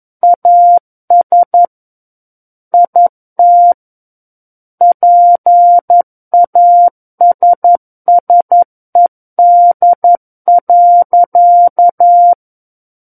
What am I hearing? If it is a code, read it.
Morse code, heard as a keyed tone, about 11 wpm, 700 Hz. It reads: AS IT PASSED.
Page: SKCC - War of the Worlds - 02-Chapter 2 - 11 WPM